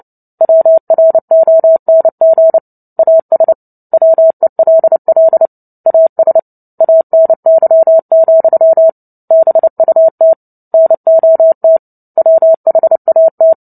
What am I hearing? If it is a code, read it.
Morse code, heard as a keyed tone, about 30 wpm, 650 Hz. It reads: WRONG AS WELL AS ANY, BUT NOT WHAT
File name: Book Two / 03-Chapter 3 - 29 WPM